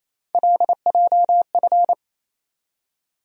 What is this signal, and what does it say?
Morse code, heard as a keyed tone, about 28 wpm, 700 Hz. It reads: LJF
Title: USA Call Sign Postfixes - 3 Letters 28wpm